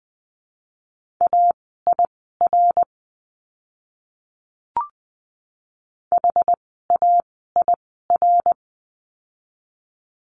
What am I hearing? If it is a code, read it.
Morse code, heard as a keyed tone, about 20 wpm, spaced as for 14 wpm, 700 Hz. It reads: AIR HAIR
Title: 101 Lesson 17f - Letter u - Words 20wpm